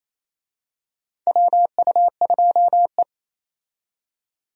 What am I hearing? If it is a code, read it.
Morse code, heard as a keyed tone, about 28 wpm, 700 Hz. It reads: WU2E
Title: USA Call Signs - 4 Characters - 28wpm